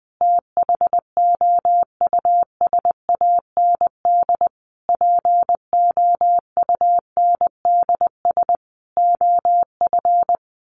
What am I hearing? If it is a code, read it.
Morse code, heard as a keyed tone, about 20 wpm, 700 Hz. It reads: THOUSAND POUNDS OF
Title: A Sound Of Thunder 20wpm